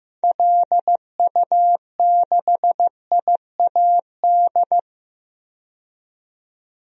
Morse code, heard as a keyed tone, about 16 wpm, 700 Hz. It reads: LU6IAD
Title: DXCC-100 Call Signs - 6 Characters - 15wpm